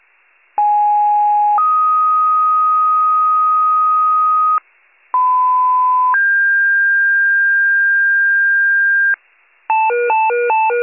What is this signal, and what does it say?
Two-tone page, heard as a one-second tone, then a three-second tone, about 830 then 1250 Hz, about 1010 then 1600 Hz, with noise. It is heard from 0.6 to 9.2 s.